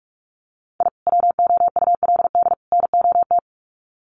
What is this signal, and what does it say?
Morse code, heard as a keyed tone, about 44 wpm, 700 Hz. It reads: I WOULD NOT
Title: Sets of 3 Words 45wpm